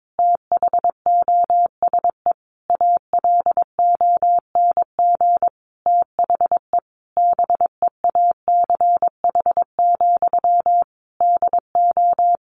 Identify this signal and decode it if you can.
Morse code, heard as a keyed tone, about 22 wpm, 700 Hz. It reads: THOSE ALONG THE BEACH, DO
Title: Sentences from Top 1500 Words 22wpm